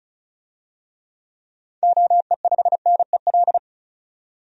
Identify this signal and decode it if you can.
Morse code, heard as a keyed tone, about 34 wpm, 700 Hz. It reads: OE5NEL
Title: DXCC-100 Call Signs - 6 Characters - 35wpm